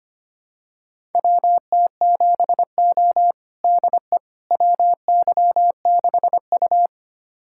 Morse code, heard as a keyed tone, about 25 wpm, 700 Hz. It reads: WT7O DE WY6U
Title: USA Callsign DE Callsign - 4 Characters - 25wpm